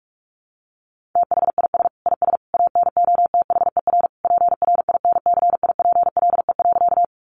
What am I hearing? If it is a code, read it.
Morse code, heard as a keyed tone, about 44 wpm, 700 Hz. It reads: THIS IS ANOTHER PRINCIPLE.
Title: Sentences from Top 1400 Words 45wpm